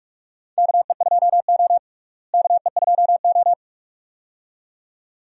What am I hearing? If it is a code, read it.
Morse code, heard as a keyed tone, about 43 wpm, 700 Hz. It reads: KE2O KE2O
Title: USA Call Signs - 4 Characters - Repeated 2x - 45wpm